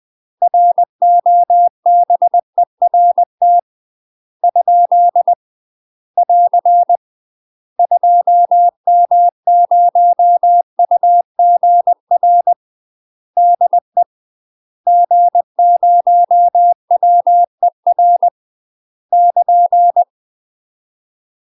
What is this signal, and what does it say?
Morse code, heard as a keyed tone, about 20 wpm, 700 Hz. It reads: ROBERT ? <AR> 2M0UGR DE G0WER <KN>
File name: UK MASTER QSOs TRAINING / 20WPM / UK MASTER QSOs 0047